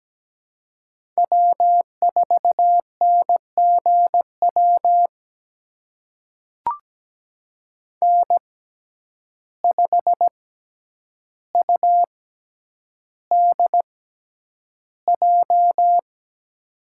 Morse code, heard as a keyed tone, about 17 wpm, spaced as for 9 wpm, 700 Hz. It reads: W4NGW N5UDJ
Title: USA Call Signs - 5 Characters - 6x Character Spacing - 17wpm